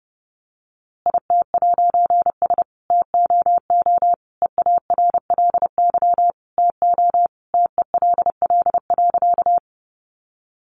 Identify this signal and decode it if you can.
Morse code, heard as a keyed tone, about 30 wpm, 700 Hz. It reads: IT'S TOO EARLY TO TELL.
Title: A Sound Of Thunder 30wpm